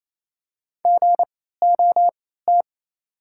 Morse code, heard as a keyed tone, about 28 wpm, spaced as for 16 wpm, 700 Hz. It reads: GOT
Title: 3x Character Spacing - Top 500 Words - 28wpm